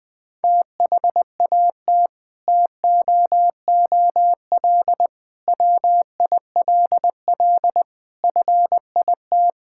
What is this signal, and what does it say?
Morse code, heard as a keyed tone, about 19 wpm, 700 Hz. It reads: THAT TOOL WILL FIT
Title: Sentences from Top 1000 Words - Review Part 6 - 20wpm